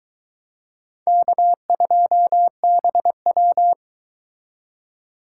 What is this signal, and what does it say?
Morse code, heard as a keyed tone, about 23 wpm, 700 Hz. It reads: K2BW